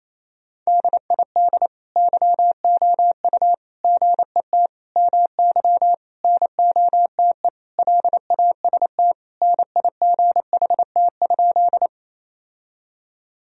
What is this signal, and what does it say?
Morse code, heard as a keyed tone, about 28 wpm, 700 Hz. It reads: DID YOU GET MY NOTE LAST NIGHT?
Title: Sentences from Top 1000 Words - Review Part 2 - 28wpm